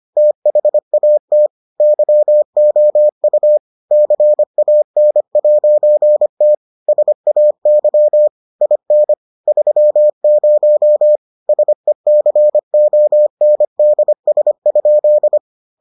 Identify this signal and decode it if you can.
Morse code, heard as a keyed tone, about 25 wpm, 600 Hz. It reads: THAT YOU CAN'T SAY IN 30 SECONDS?